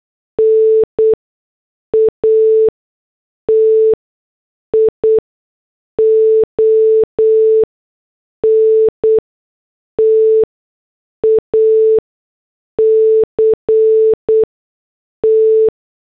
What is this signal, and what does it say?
Morse code, heard as a keyed tone, about 8 wpm, spaced as for 6 wpm, 440 Hz. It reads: NATION TACT